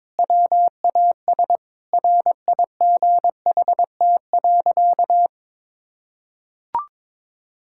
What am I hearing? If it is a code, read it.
Morse code, heard as a keyed tone, about 22 wpm, 700 Hz. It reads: WAS RIGHT.
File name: practice / 22wpm / Sentences with 4-Word Phrases - 22wpm